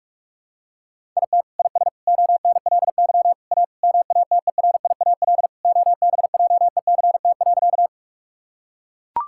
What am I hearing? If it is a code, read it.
Morse code, heard as a keyed tone, about 43 wpm, 700 Hz. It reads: IT IS ONLY A MATERIAL OBJECT.
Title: Sentences from Top 1000 Words - Review Part 5 - 45wpm